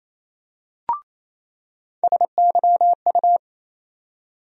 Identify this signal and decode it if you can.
Morse code, heard as a keyed tone, about 28 wpm, 700 Hz. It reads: SYU
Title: USA Call Sign Postfixes - 3 Letters 28wpm